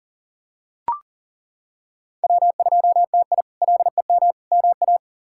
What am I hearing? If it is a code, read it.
Morse code, heard as a keyed tone, about 40 wpm, 700 Hz. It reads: W2TI LEM MA
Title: K1USN SST QSO Practice - 40wpm